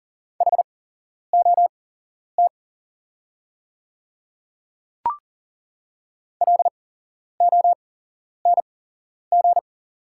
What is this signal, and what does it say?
Morse code, heard as a keyed tone, about 40 wpm, spaced as for 11 wpm, 700 Hz. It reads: HOT LONG
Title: 8x Character Spacing - Top 200 Words - 40wpm